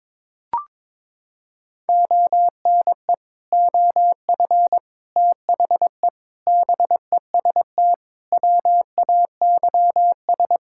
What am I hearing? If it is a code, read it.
Morse code, heard as a keyed tone, about 22 wpm, 700 Hz. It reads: ONE OF THE BEST WAYS
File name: practice / 22wpm / Sets of 6 Words - 22wpm